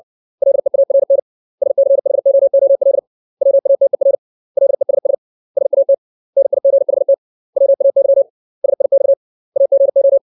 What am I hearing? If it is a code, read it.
Morse code, heard as a keyed tone, about 60 wpm, 550 Hz. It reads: LEARN SCHOOL WATER LESS SAT NIGHT WAY SIX AGO
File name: 632 words / 632 Words 60WPM